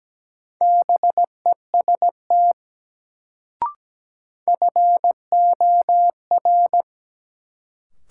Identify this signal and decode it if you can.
Morse code, heard as a keyed tone, about 17 wpm, 700 Hz. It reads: BEST FOR
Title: Sets of 2 Words - New Word Drill - 17wpm